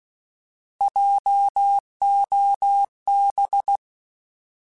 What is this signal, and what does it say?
Morse code, heard as a keyed tone, about 16 wpm, 800 Hz. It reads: JOB
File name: Alphabetical / Alphabetical mp3s / Js @15